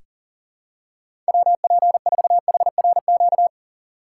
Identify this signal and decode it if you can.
Morse code, heard as a keyed tone, about 40 wpm, 700 Hz. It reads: WP4HRQ
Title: USA Call Signs - 6 Characters - 4x Character Spacing - 40wpm